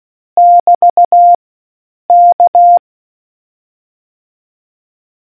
Morse code, heard as a keyed tone, about 16 wpm, 700 Hz. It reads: <BT> K